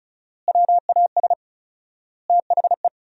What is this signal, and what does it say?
Morse code, heard as a keyed tone, about 34 wpm, 700 Hz. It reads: WAS THE
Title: 4x Word-Spacing - Sentences from Top 200 Words - 35wpm